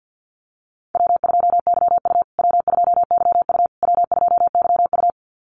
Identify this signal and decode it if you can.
Morse code, heard as a keyed tone, about 50 wpm, 700 Hz. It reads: W2YU W2YU W2YU
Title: USA Call Signs Repeated 3x 50wpm